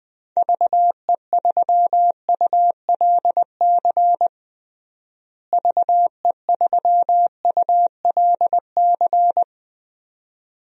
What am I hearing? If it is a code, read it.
Morse code, heard as a keyed tone, about 20 wpm, 700 Hz. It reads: VE3ULC VE3ULC
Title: DXCC-100 Call Signs - 6 Characters - Repeated 2x - 20wpm